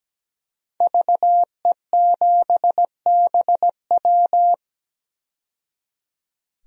Morse code, heard as a keyed tone, about 17 wpm, 700 Hz. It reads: VE7BW
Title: DXCC-100 Call Signs - 5 Characters - 17wpm